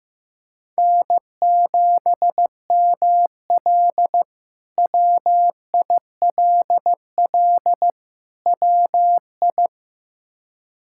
Morse code, heard as a keyed tone, about 15 wpm, 700 Hz. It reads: N7ML WILL WI
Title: K1USN SST QSO Practice - 15wpm